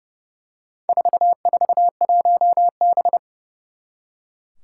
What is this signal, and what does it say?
Morse code, heard as a keyed tone, about 31 wpm, 700 Hz. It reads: 441B
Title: Four Challenging Letter-Number Combinations 30wpm